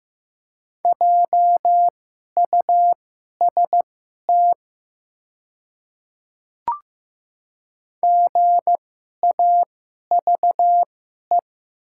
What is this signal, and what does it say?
Morse code, heard as a keyed tone, about 15 wpm, spaced as for 11 wpm, 700 Hz. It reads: JUST GAVE
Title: 2x Character Spacing - Top 500 Words - 15wpm